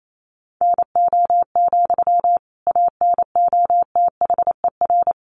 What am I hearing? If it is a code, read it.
Morse code, heard as a keyed tone, about 28 wpm, 700 Hz. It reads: NO, ANOTHER
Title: Sentences from Top 600 Words 28wpm